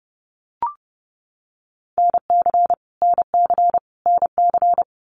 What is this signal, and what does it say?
Morse code, heard as a keyed tone, about 30 wpm, 700 Hz. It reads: NC NC NC
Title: US States Repeated 3x 30wpm